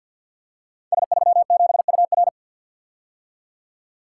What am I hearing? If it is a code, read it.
Morse code, heard as a keyed tone, about 45 wpm, 700 Hz. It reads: S37UR
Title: Code Groups - 5 Letter:Numbers - 50wpm